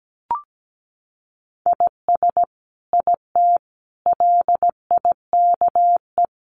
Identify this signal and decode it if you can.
Morse code, heard as a keyed tone, about 17 wpm, 700 Hz. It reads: IS IT LIKE
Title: Sentences from Top 100 Words 17wpm